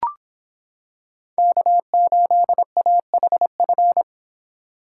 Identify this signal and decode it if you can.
Morse code, heard as a keyed tone, about 25 wpm, 700 Hz. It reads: K8AHF